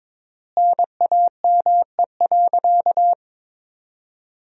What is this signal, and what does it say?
Morse code, heard as a keyed tone, about 22 wpm, 700 Hz. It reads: NAME.
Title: 3x Word-Spacing - Sentences from Top 200 Words - 22wpm